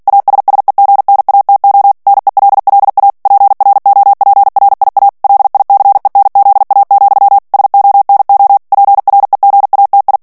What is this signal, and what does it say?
Morse code, heard as a keyed tone, about 48 wpm, 800 Hz. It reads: ASSEGNATO NELLA PROPRIA LICENZA, SONO PREGATI
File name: ITALIEN texte clair 2 vitesse 45 mots minute